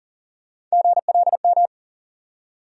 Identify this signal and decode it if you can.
Morse code, heard as a keyed tone, about 40 wpm, 700 Hz. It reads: GLM
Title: USA Call Sign Postfixes - 3 Letters 40wpm